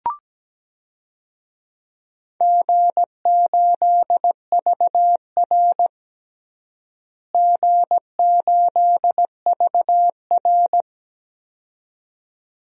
Morse code, heard as a keyed tone, about 17 wpm, 700 Hz. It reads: G8VR G8VR
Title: DXCC-100 Call Signs - 4 Characters - Repeated 2x - 17wpm